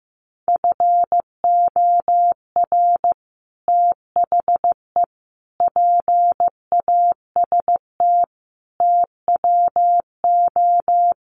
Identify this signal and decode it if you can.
Morse code, heard as a keyed tone, about 15 wpm, 700 Hz. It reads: FOR THE PAST TWO
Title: Sets of 5 Words Encore2 15wpm